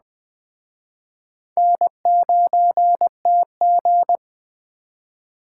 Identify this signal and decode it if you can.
Morse code, heard as a keyed tone, about 21 wpm, 700 Hz. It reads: N9TG
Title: Speed Racing - USA 1x2 Call Signs 17wpm